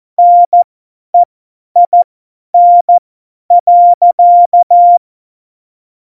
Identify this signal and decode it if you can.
Morse code, heard as a keyed tone, about 14 wpm, spaced as for 10 wpm, 700 Hz. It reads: NEIN.
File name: Morsekurs / Witze / Witz 03. November 2020, 50 F70